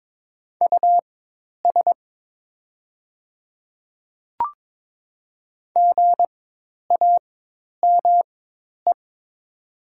Morse code, heard as a keyed tone, about 22 wpm, spaced as for 10 wpm, 700 Hz. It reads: US GAME
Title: 4x Character Spacing - Top 500 Words - 22wpm